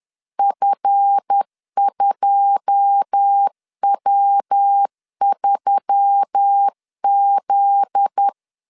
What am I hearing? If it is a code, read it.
Morse code, heard as a keyed tone, about 10 wpm, 800 Hz. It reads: F2W3Z
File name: CW / DET / 1 9 1500C S B 10wpm 4439